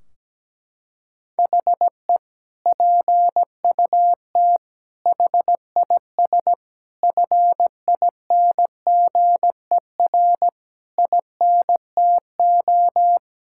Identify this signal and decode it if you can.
Morse code, heard as a keyed tone, about 17 wpm, 700 Hz. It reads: HE PUT HIS FINGER INTO
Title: Sentences from Top 1600 Words 17wpm